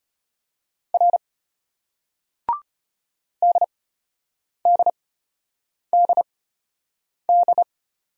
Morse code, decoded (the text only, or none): R DDDD